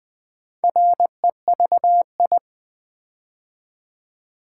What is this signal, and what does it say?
Morse code, heard as a keyed tone, about 20 wpm, 700 Hz. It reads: REVI
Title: Build A Word - 4 Letters At A Time - 8-Letter Words - 20wpm